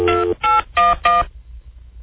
DTMF, heard as telephone touch tones, tones of about 180 ms. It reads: 2811